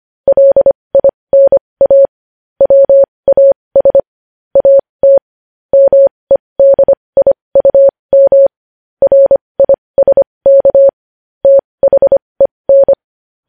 Morse code, heard as a keyed tone, about 25 wpm, spaced as for 20 wpm, 550 Hz. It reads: LINA WAS AT MEDIUM RISK THEN
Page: news-headlines-in-morse-code-at-20-wpm